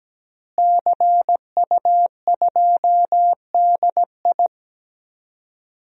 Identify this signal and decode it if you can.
Morse code, heard as a keyed tone, about 17 wpm, 700 Hz. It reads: CU2DI
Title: DXCC-100 Call Signs - 5 Characters - Repeated 2x - 17wpm